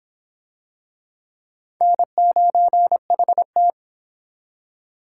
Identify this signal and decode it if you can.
Morse code, heard as a keyed tone, about 24 wpm, 700 Hz. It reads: N9HT